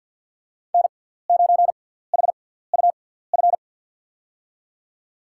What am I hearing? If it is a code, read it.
Morse code, heard as a keyed tone, about 53 wpm, 700 Hz. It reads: N 9 H U F
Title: USA Call Signs - 5 Characters - 6x Character Spacing - 50wpm